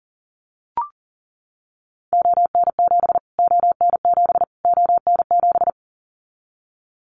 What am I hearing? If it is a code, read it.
Morse code, heard as a keyed tone, about 40 wpm, 700 Hz. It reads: ON7 ON7 ON7
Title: Callsign Prefixes for Top 25 DXCC - Repeated 3x - 40wpm